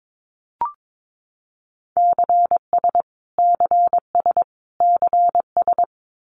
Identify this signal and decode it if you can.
Morse code, heard as a keyed tone, about 22 wpm, 700 Hz. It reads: CS CS CS
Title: Abbreviations Repeated 3x - 22wpm